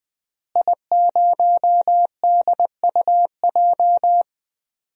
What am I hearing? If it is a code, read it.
Morse code, heard as a keyed tone, about 20 wpm, 700 Hz. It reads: I0DUJ
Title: DXCC-100 Call Signs - 5 Characters - Repeated 2x - 20wpm